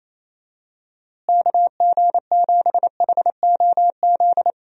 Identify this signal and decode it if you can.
Morse code, heard as a keyed tone, about 28 wpm, 700 Hz. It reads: KG7HOZ
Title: USA Call Signs - 6 Characters - Repeated 3x - 28wpm